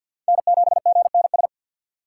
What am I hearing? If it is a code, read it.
Morse code, heard as a keyed tone, about 50 wpm, 700 Hz. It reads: N7GNS
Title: USA Call Signs - 6x Character Spacing - 50wpm